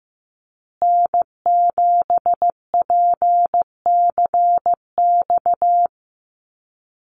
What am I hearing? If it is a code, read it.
Morse code, heard as a keyed tone, about 15 wpm, 700 Hz. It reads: N7PCX
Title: USA Call Signs Repeated 3x 15wpm